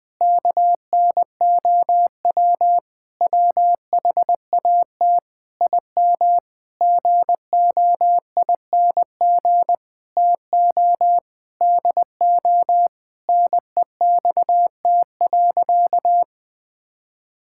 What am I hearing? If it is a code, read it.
Morse code, heard as a keyed tone, about 20 wpm, 700 Hz. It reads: KNOW WHAT IM GOING TO DO NEXT.
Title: Sentences with 8-Word Phrases - Spoken Beginning - 20wpm